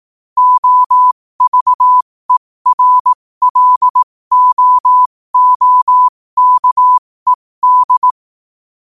Morse code, heard as a keyed tone, about 18 wpm, spaced as for 15 wpm, 1000 Hz. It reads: OVERLOOKED
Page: 15 wpm morse code quotes